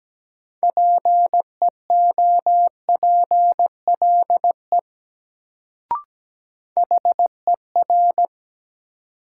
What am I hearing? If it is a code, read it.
Morse code, heard as a keyed tone, about 17 wpm, 700 Hz. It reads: PEOPLE HER
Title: Sets of 2 Words - New Word Drill - 17wpm